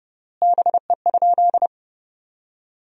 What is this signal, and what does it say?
Morse code, heard as a keyed tone, about 30 wpm, 700 Hz. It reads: BE?
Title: 3x Word-Spacing - Sentences from Top 100 Words - 30wpm